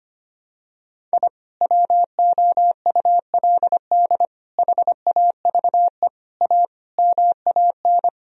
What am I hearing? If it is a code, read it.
Morse code, heard as a keyed tone, about 25 wpm, 700 Hz. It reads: I WOULD HAVE A MAN
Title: Sentences from Top 1000 Words - Review Part 2 - 25wpm